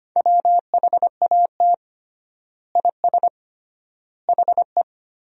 Morse code, heard as a keyed tone, about 25 wpm, 700 Hz. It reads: WHAT IS HE
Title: 3x Word-Spacing - Sentences from Top 100 Words - 25wpm